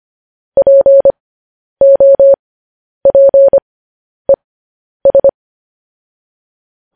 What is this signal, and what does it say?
Morse code, heard as a keyed tone, about 25 wpm, spaced as for 10 wpm, 550 Hz. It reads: POPES
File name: podcast / News headlines-1x-25x10